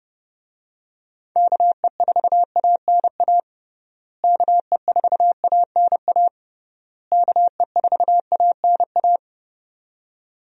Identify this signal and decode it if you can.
Morse code, heard as a keyed tone, about 30 wpm, 700 Hz. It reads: KE4ANA KE4ANA KE4ANA
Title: USA Call Signs - 6 Characters - Repeated 3x - 30wpm